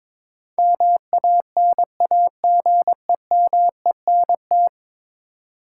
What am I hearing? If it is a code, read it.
Morse code, heard as a keyed tone, about 22 wpm, 700 Hz. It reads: MANAGEMENT
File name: practice / 22wpm / 10 Letter Words - First 3 Letters Given - 22wpm